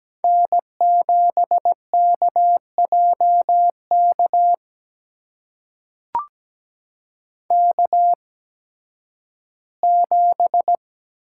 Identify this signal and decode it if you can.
Morse code, heard as a keyed tone, about 17 wpm, spaced as for 11 wpm, 700 Hz. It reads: N7KJK K7
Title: USA Call Signs - 5 Characters - 8x Character Spacing - 17wpm